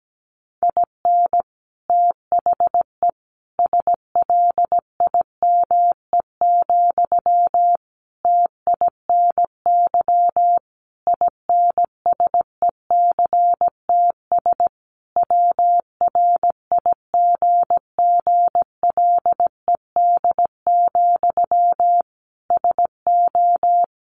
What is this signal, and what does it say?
Morse code, heard as a keyed tone, about 17 wpm, 700 Hz. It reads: IN THE SLIME, TINY INSECTS WRIGGLED, SO
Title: A Sound Of Thunder 17wpm